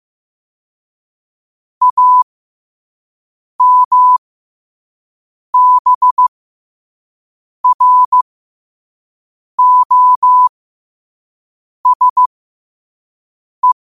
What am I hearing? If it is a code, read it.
Morse code, heard as a keyed tone, about 15 wpm, spaced as for 5 wpm, 1000 Hz. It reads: AMBROSE
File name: qotdfiles / 5wpm